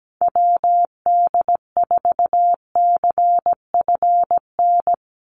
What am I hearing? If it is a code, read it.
Morse code, heard as a keyed tone, about 17 wpm, 700 Hz. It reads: WD4CFN
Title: NA SOTA Chaser Callsigns 17wpm